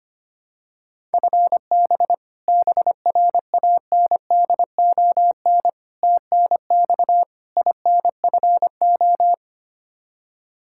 Morse code, heard as a keyed tone, about 25 wpm, 700 Hz. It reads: FB BRANDON TNX INFO
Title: Practice QSO - Exchange 3 - More Info - 25wpm